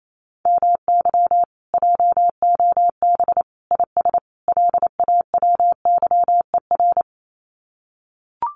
Morse code, heard as a keyed tone, about 28 wpm, 700 Hz. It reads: MY JOB IS LAWYER
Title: Common QSO Phrases Encore 28wpm